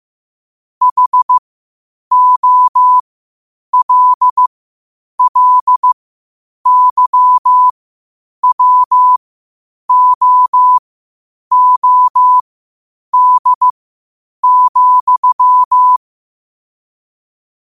Morse code, heard as a keyed tone, about 15 wpm, spaced as for 8 wpm, 1000 Hz. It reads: HOLLYWOOD,
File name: qotdfiles / 8wpm